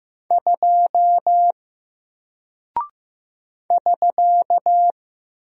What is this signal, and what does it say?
Morse code, heard as a keyed tone, about 15 wpm, 700 Hz. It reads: 2 <SK>
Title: Single Character (Letter:Number:Punctuation:Prosign) Alt 15wpm